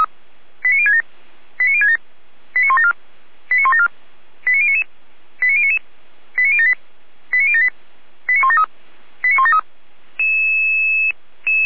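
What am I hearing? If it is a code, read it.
Five-tone selective call (ZVEI), heard as quick runs of five tones, tones of about 70 ms. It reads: 89067 89067 89174 89174 89099 89099 89078 89078 89173 89173
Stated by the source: ZVEI five-tone call received by SDR, from feuerwehr-orlamuende (fire brigade).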